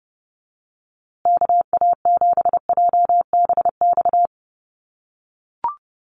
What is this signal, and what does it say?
Morse code, heard as a keyed tone, about 30 wpm, 700 Hz. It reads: KA7JBX
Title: USA Call Signs Repeated 3x 30wpm